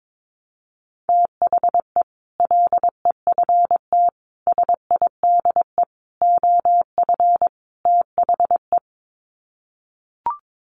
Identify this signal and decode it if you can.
Morse code, heard as a keyed tone, about 22 wpm, 700 Hz. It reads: THE LEFT SIDE OF THE
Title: Sets of 5 Words Encore 22wpm